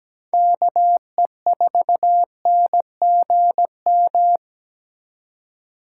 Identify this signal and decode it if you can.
Morse code, heard as a keyed tone, about 17 wpm, 700 Hz. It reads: KE4NGM
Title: USA Call Signs - 6 Characters - 8x Character Spacing - 17wpm